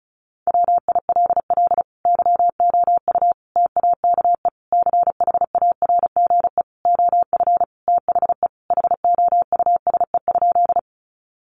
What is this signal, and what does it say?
Morse code, heard as a keyed tone, about 34 wpm, 700 Hz. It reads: WILL YOU TAKE CHARGE OF THE HOUSE?
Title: Sentences from Top 900 Words 35wpm